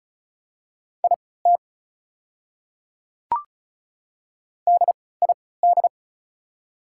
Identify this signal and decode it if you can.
Morse code, heard as a keyed tone, about 36 wpm, spaced as for 20 wpm, 690 Hz. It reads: IT DID
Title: 3x Character Spacing - Top 100 Words - 35wpm